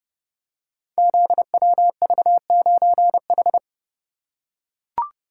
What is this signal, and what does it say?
Morse code, heard as a keyed tone, about 28 wpm, 700 Hz. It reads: ZWV9H
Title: Code Groups - 5 Letter:Numbers - 30wpm